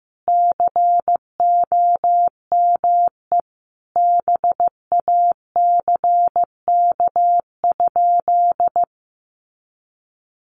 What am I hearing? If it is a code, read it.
Morse code, heard as a keyed tone, about 15 wpm, 700 Hz. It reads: COME BACK?
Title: Sentences from Top 200 Words 15wpm